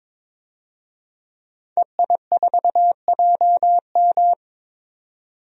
Morse code, heard as a keyed tone, about 22 wpm, 700 Hz. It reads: EI4JM
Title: DXCC-100 Call Signs - 5 Characters - 22wpm